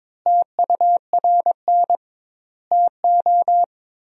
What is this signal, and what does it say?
Morse code, heard as a keyed tone, about 22 wpm, 700 Hz. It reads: TURN TO
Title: 2x Word-Spacing - Sentences from Top 200 Words - 22wpm